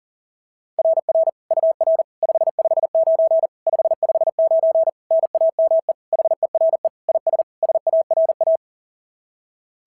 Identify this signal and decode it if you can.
Morse code, heard as a keyed tone, about 40 wpm, 690 Hz. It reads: RR UR 559 559 NAME HERE IS SARA
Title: SOTA QSO Practice 40wpm